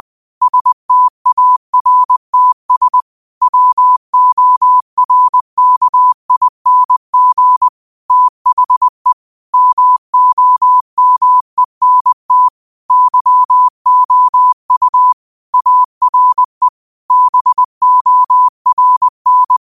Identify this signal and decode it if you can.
Morse code, heard as a keyed tone, about 20 wpm, 1000 Hz. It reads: STARTS WORKING THE MOMENT YOU ARE BORN
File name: 20wpm